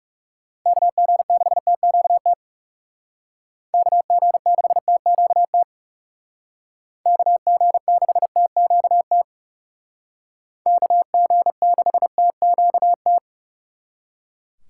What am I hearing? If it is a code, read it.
Morse code, heard as a keyed tone, about 37 wpm, 700 Hz. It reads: KG6TQT KG6TQT KG6TQT KG6TQT
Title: Speed Racing - USA 2x3 Call Signs 30wpm